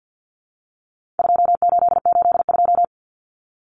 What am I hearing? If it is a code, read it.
Morse code, heard as a keyed tone, about 50 wpm, 700 Hz. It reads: 2882
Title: Common Cut Numbers - 4-Digit 50wpm